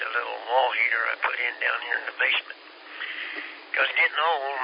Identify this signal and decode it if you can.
Emergency Alert System (SAME) tones, heard as harsh buzzing bursts, with speech. It is heard from 0.7 to 3.5 s.